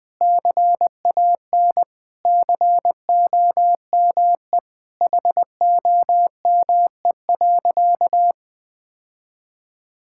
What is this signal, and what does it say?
Morse code, heard as a keyed tone, about 19 wpm, 700 Hz. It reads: CAN COME HOME.
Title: Sentences with 3-Word Phrases - Spoken Beginning - 20wpm